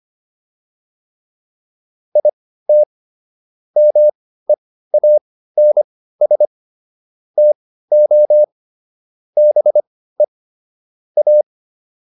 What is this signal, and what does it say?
Morse code, heard as a keyed tone, about 25 wpm, spaced as for 15 wpm, 600 Hz. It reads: IT MEANS TO BE A